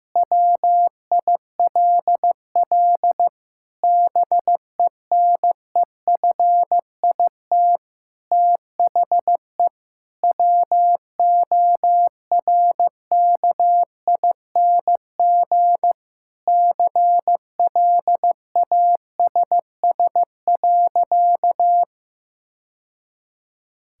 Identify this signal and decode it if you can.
Morse code, heard as a keyed tone, about 15 wpm, 700 Hz. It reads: WILL BENEFIT THE WORKING CLASS.
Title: Sentences from Top 1000 Words - Review Part 6 - 15wpm